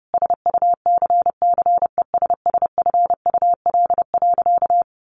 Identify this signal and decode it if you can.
Morse code, heard as a keyed tone, about 30 wpm, 700 Hz. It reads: SUCCESSFUL.
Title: Sentences from Top 1500 Words 30wpm